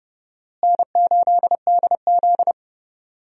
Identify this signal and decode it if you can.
Morse code, heard as a keyed tone, about 32 wpm, 700 Hz. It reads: N8DZ
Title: USA Call Signs - 4 Characters - Repeated 2x - 30wpm